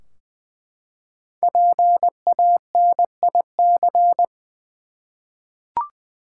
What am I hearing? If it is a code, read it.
Morse code, heard as a keyed tone, about 20 wpm, 700 Hz. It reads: PANIC
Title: Five Letter Words - 20wpm